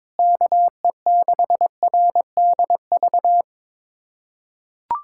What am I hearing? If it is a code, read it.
Morse code, heard as a keyed tone, about 22 wpm, 700 Hz. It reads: KE6RDV
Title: USA Callsign DE Callsign - 6 Characters - 22wpm